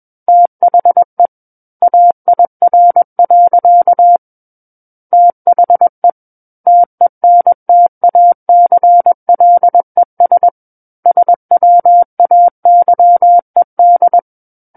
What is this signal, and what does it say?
Morse code, heard as a keyed tone, about 21 wpm, 700 Hz. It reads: THE AIR. THE TENTACLES SWAYED
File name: Book One / 12-Chapter 12 - 21 WPM